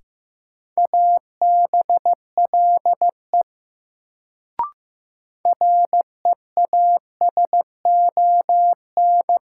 Morse code, heard as a keyed tone, about 15 wpm, 700 Hz. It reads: ABLE REASON
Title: Sentences from Top 700 Words - New Word Drill - 15wpm